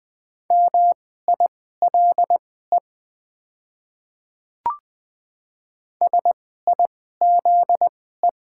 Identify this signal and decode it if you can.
Morse code, heard as a keyed tone, about 20 wpm, spaced as for 14 wpm, 700 Hz. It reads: MILE SIZE
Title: 2x Character Spacing - Top 500 Words - 20wpm